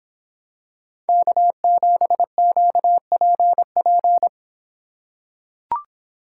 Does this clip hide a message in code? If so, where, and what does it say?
Yes: Morse code from 1.1 to 4.3 s, heard as a keyed tone, about 26 wpm, 700 Hz. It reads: K7QPP